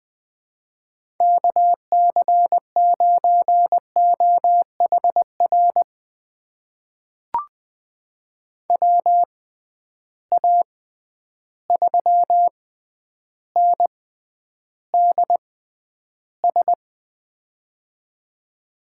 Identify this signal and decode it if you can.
Morse code, heard as a keyed tone, about 20 wpm, spaced as for 10 wpm, 700 Hz. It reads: KC9OHR WA3NDS
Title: USA Call Signs - 6 Characters - 6x Character Spacing - 20wpm